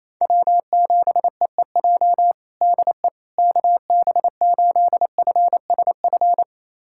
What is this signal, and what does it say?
Morse code, heard as a keyed tone, about 28 wpm, 700 Hz. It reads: W7EEJ DE KB8FSF
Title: USA Callsign DE Callsign 28wpm